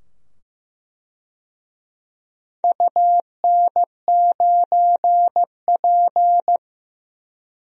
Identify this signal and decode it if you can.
Morse code, heard as a keyed tone, about 15 wpm, 700 Hz. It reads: UN9P
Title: DXCC-100 Call Signs - 4 Characters - 15wpm